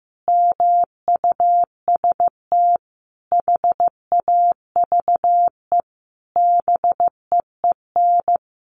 Morse code, heard as a keyed tone, about 15 wpm, 700 Hz. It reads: MUST HAVE BEEN
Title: A Sound Of Thunder 15wpm